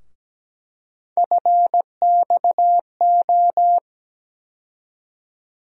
Morse code, heard as a keyed tone, about 17 wpm, 700 Hz. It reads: FXO
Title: USA Call Sign Postfixes - 3 Letters 17wpm